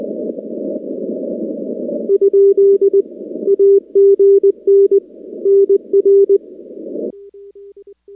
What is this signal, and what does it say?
Morse code, heard as a keyed tone, about 20 wpm, 400 Hz. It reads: ? AGN NR
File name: contests / 2019-cw-wpx-kt5j-k5pi- -2330 0-13